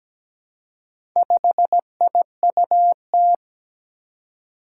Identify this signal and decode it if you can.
Morse code, heard as a keyed tone, about 17 wpm, 700 Hz. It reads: 5IUT
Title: Four Challenging Letter-Number Combinations 17wpm